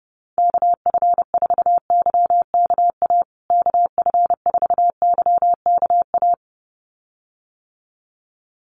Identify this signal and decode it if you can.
Morse code, heard as a keyed tone, about 30 wpm, 700 Hz. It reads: KF4YKA KF4YKA
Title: USA Call Signs Repeated 2x 30wpm